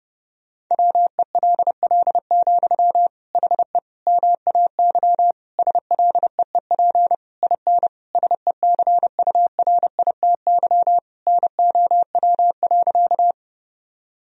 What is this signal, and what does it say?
Morse code, heard as a keyed tone, about 28 wpm, 700 Hz. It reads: WELL, HE MAY SLEEP IN SECURITY NOW.
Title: Sentences from Top 1000 Words - Review Part 4 - 30wpm